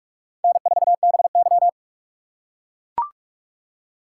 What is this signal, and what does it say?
Morse code, heard as a keyed tone, about 44 wpm, 700 Hz. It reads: N4BY